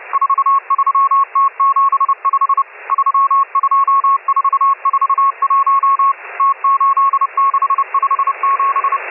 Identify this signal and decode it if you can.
Morse code, heard as a keyed tone, about 30 wpm, 1100 Hz, with radio noise. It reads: 43T75 32441 T8657
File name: M12- -1010utc